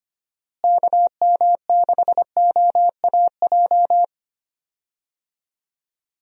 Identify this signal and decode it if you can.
Morse code, heard as a keyed tone, about 25 wpm, 700 Hz. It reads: KM6OAJ
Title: USA Call Signs - 6 Characters - Repeated 3x - 25wpm